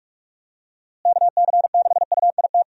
Morse code, heard as a keyed tone, about 41 wpm, 700 Hz. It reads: KC6UIT
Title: USA Call Signs - 6 Characters - Repeated 2x - 45wpm